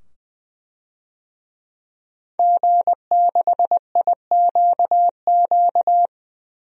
Morse code, heard as a keyed tone, about 20 wpm, 700 Hz. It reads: G6IQQ
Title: DXCC-100 Call Signs - 5 Characters - 20wpm